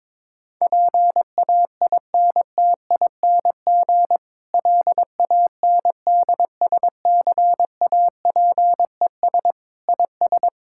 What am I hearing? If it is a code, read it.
Morse code, heard as a keyed tone, about 22 wpm, 700 Hz. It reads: PAINTING LANDSCAPES IS
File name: Difficult Sentences with Hints - 22wpm